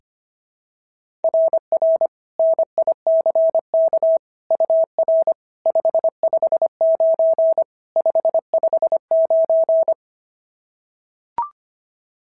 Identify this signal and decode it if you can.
Morse code, heard as a keyed tone, about 24 wpm, 650 Hz. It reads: RR NICK UR 559 559
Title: SOTA QSO Practice 25wpm